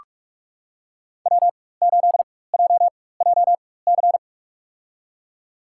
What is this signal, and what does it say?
Morse code, heard as a keyed tone, about 46 wpm, spaced as for 21 wpm, 700 Hz. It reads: W8JJC
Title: USA Call Signs - 5 Characters - 4x Character Spacing - 45wpm